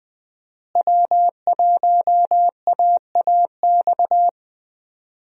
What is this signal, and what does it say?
Morse code, heard as a keyed tone, about 20 wpm, 700 Hz. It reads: W1AAX
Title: USA Call Signs - 5 Characters - Repeated 2x - 20wpm